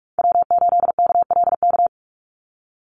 Morse code, heard as a keyed tone, about 44 wpm, 700 Hz. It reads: W8KLK
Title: USA Call Signs 45wpm